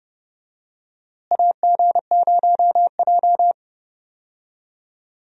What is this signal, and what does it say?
Morse code, heard as a keyed tone, about 32 wpm, 700 Hz. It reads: AG0J
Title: USA Call Signs - 4 Characters - Repeated 2x - 30wpm